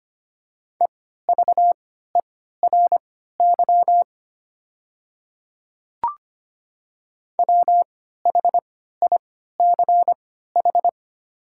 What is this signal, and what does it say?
Morse code, heard as a keyed tone, about 25 wpm, spaced as for 14 wpm, 700 Hz. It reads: EVERY WHICH
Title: 3x Character Spacing - Top 200 Words - 25wpm